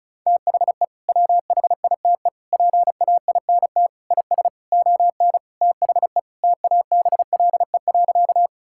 Morse code, heard as a keyed tone, about 34 wpm, 700 Hz. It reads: THE WHITE PAINT IS ON THE TABLE.
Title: Sentences from Top 1000 Words - Review Part 6 - 35wpm